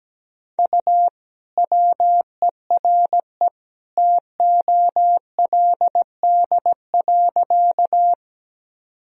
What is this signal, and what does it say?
Morse code, heard as a keyed tone, about 17 wpm, 700 Hz. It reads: U WERE TOLD.
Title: Sentences from Top 400 Words and Abbreviations 17wpm